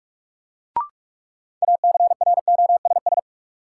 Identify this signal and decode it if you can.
Morse code, heard as a keyed tone, about 48 wpm, 700 Hz. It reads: ACROSS
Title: Sets of 2 Words - New Word Drill - 45wpm